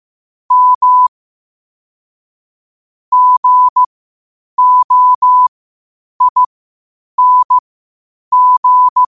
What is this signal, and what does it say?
Morse code, heard as a keyed tone, about 15 wpm, spaced as for 8 wpm, 1000 Hz. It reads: M GOING
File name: qotdfiles / 8wpm